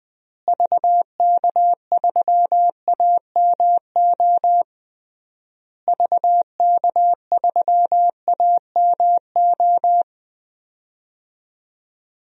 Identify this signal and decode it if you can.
Morse code, heard as a keyed tone, about 20 wpm, 700 Hz. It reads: VK3AMO VK3AMO
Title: DXCC-100 Call Signs - 6 Characters - Repeated 3x - 20wpm